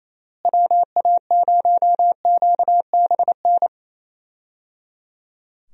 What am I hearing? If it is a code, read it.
Morse code, heard as a keyed tone, about 28 wpm, 700 Hz. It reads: WA0QBN
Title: USA Call Signs - 6 Characters - Repeated 3x - 28wpm